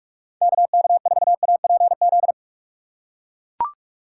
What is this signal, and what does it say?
Morse code, heard as a keyed tone, about 44 wpm, 700 Hz. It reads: KK4APZ